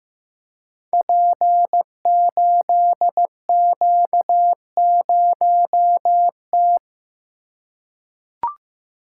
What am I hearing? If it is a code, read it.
Morse code, heard as a keyed tone, about 15 wpm, 700 Hz. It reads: P8Q0T
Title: Code Groups - 5 Letter:Numbers - 15wpm